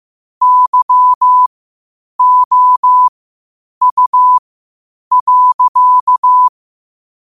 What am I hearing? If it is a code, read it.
Morse code, heard as a keyed tone, about 15 wpm, spaced as for 8 wpm, 1000 Hz. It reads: YOU.
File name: qotdfiles / 8wpm